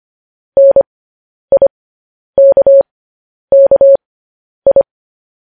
Morse code, heard as a keyed tone, about 25 wpm, spaced as for 10 wpm, 550 Hz. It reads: NIKKI